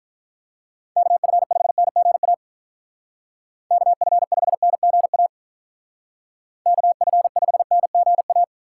Morse code, heard as a keyed tone, about 46 wpm, 700 Hz. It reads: KF5NGA KF5NGA KF5NGA